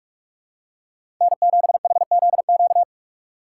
Morse code, heard as a keyed tone, about 44 wpm, 690 Hz. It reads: N7HZQ
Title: USA Call Signs - 5 Characters - 6x Character Spacing - 45wpm